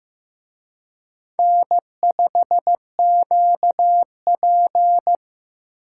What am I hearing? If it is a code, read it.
Morse code, heard as a keyed tone, about 18 wpm, 700 Hz. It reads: N5QP
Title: Speed Racing - USA 1x2 Call Signs 15wpm